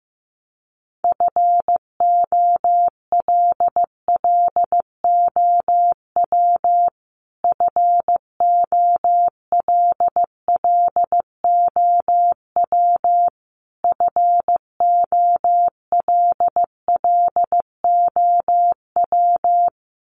Morse code, heard as a keyed tone, about 15 wpm, 700 Hz. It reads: FOLLOW FOLLOW FOLLOW
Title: Top 200 Words Repeated 15wpm